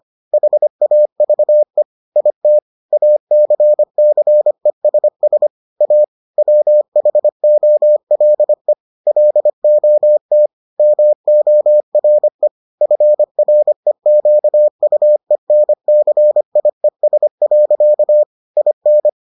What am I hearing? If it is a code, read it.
Morse code, heard as a keyed tone, about 25 wpm, 600 Hz. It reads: HAVE IT ACCESS A WHOLE LOT MORE FREQUENCIES. IN